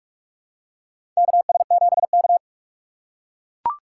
Morse code, heard as a keyed tone, about 44 wpm, 700 Hz. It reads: KS7K
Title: USA Call Signs - 4 Characters - 3x Character Spacing - 45wpm